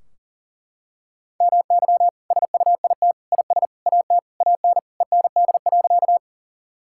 Morse code, heard as a keyed tone, about 37 wpm, 700 Hz. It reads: MY SUIT IS AT AN END.
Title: Sentences from Top 1600 Words 40wpm